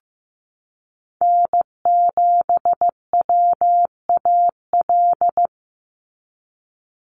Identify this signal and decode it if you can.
Morse code, heard as a keyed tone, about 15 wpm, 700 Hz. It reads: N7WAL
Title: USA Call Signs 15wpm